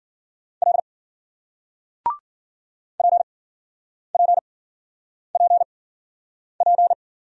Morse code, heard as a keyed tone, about 53 wpm, spaced as for 9 wpm, 700 Hz. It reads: L PPPP